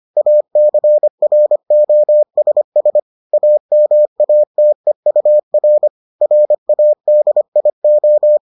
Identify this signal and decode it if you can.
Morse code, heard as a keyed tone, about 25 wpm, 600 Hz. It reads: ACROSS AMATEUR RADIO